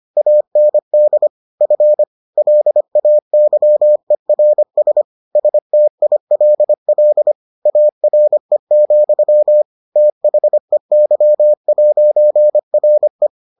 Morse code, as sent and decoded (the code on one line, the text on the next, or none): .- -. -.. / ..-. / .-.. .- -.-- . .-. ... / ... - .. .-.. .-.. / .- .-. --..-- / - .... -.-- .----. .-. .
AND F LAYERS STILL ARE, THEY'RE